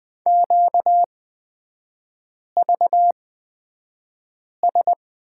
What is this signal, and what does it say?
Morse code, heard as a keyed tone, about 20 wpm, spaced as for 5 wpm, 700 Hz. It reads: QVS
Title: 250 Lesson 40e - BK - Three Characters 20wpm